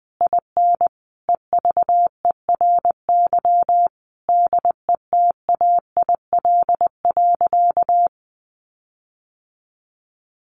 Morse code, heard as a keyed tone, about 20 wpm, 700 Hz. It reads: IN EVERY DETAIL.
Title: Sentences from Top 1500 Words 20wpm